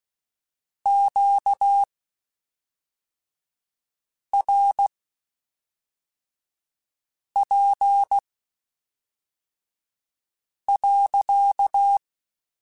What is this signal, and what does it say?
Morse code, heard as a keyed tone, about 16 wpm, spaced as for 3 wpm, 800 Hz. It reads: QRP.
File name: Q Calls / Q Calls-mp3s / Q Calls sppc @5